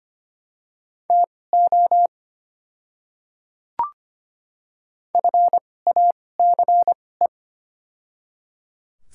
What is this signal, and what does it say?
Morse code, heard as a keyed tone, about 25 wpm, spaced as for 18 wpm, 700 Hz. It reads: TO FACE
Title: 2x Character Spacing - Top 300 Words - 25wpm